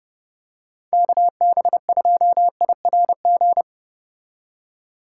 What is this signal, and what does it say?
Morse code, heard as a keyed tone, about 31 wpm, 700 Hz. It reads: KB2IRG